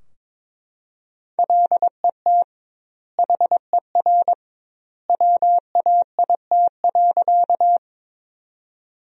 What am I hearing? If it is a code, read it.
Morse code, heard as a keyed tone, about 22 wpm, 700 Hz. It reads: LET HER WAIT.
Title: 2x Word-Spacing - Sentences from Top 500 Words - 22wpm